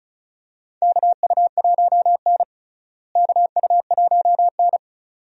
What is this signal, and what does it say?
Morse code, heard as a keyed tone, about 36 wpm, 700 Hz. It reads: KU1N KU1N
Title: USA Call Signs - 4 Characters - Repeated 2x - 35wpm